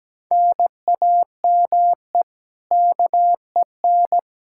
Morse code, heard as a keyed tone, about 17 wpm, 700 Hz. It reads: NAME KEN
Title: Practice QSO - Exchange 1 - Beginning - 17wpm